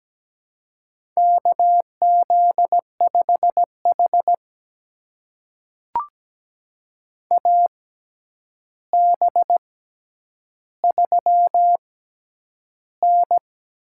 Morse code, heard as a keyed tone, about 17 wpm, spaced as for 9 wpm, 700 Hz. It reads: KZ5H AB3N
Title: USA Call Signs - 6x Character Spacing - 17wpm